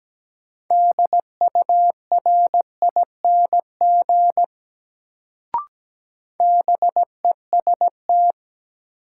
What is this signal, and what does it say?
Morse code, heard as a keyed tone, about 17 wpm, 700 Hz. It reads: DURING BEST
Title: Sets of 2 Words - New Word Drill - 17wpm